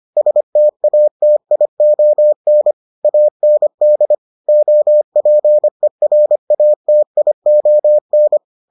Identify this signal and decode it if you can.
Morse code, heard as a keyed tone, about 25 wpm, 600 Hz. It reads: STATION AND OPERATION